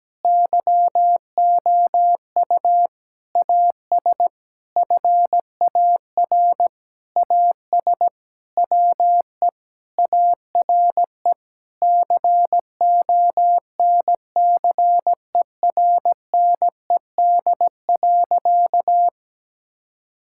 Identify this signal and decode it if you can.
Morse code, heard as a keyed tone, about 17 wpm, 700 Hz. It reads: YOU AS FAR AS WE ARE CONCERNED.
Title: Sentences from Top 1000 Words - Review Part 6 - 17wpm